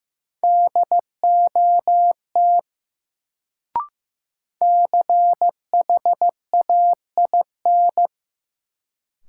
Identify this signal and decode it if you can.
Morse code, heard as a keyed tone, about 15 wpm, 700 Hz. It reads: DOT CHAIN
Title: Sentences from Top 1500 Words - New Word Drill - 15wpm